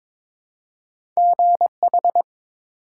Morse code, heard as a keyed tone, about 22 wpm, 700 Hz. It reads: GH
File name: practice / 22wpm / ARRL Sections v2 - 22wpm